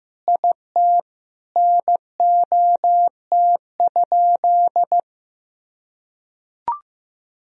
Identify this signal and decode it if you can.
Morse code, heard as a keyed tone, about 15 wpm, 700 Hz. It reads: IT NOT?
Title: Sentences from Top 1000 Words - Review Part 4 - 15wpm